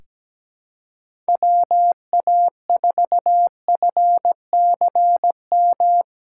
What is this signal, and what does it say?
Morse code, heard as a keyed tone, about 17 wpm, 700 Hz. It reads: WA4FCM
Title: USA Call Signs - 2x Character Spacing - 17wpm